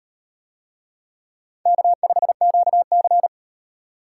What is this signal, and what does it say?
Morse code, heard as a keyed tone, about 38 wpm, 700 Hz. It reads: K5QC